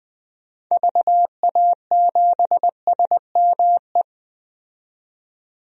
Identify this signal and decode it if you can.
Morse code, heard as a keyed tone, about 20 wpm, 700 Hz. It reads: VA7SME
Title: DXCC-100 Call Signs - 6 Characters - 20wpm